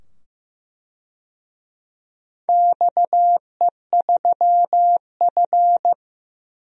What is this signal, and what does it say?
Morse code, heard as a keyed tone, about 15 wpm, 700 Hz. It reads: XE3F
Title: DXCC-100 Call Signs - 4 Characters - 15wpm